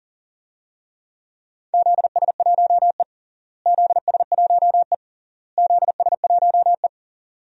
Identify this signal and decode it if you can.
Morse code, heard as a keyed tone, about 40 wpm, 700 Hz. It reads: ZS1E ZS1E ZS1E
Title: DXCC-100 Call Signs - 4 Characters - Repeated 3x - 40wpm